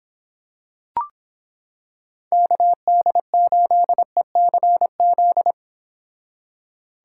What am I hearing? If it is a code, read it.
Morse code, heard as a keyed tone, about 25 wpm, 700 Hz. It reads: KD8ECZ